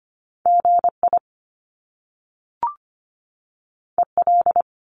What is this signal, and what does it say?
Morse code, heard as a keyed tone, about 25 wpm, 700 Hz. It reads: GI EL
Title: Top Two Letter Combinations in Words 25wpm